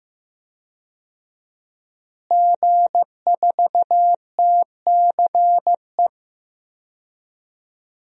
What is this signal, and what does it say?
Morse code, heard as a keyed tone, about 15 wpm, 700 Hz. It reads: G4TCE
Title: DXCC-100 Call Signs - 5 Characters - Repeated 3x - 15wpm